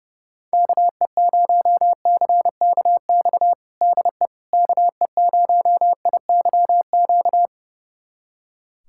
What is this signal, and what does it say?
Morse code, heard as a keyed tone, about 30 wpm, 700 Hz. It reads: KE0CKX DE KE0IYQ
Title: USA Callsign DE Callsign - 6 Characters - 30wpm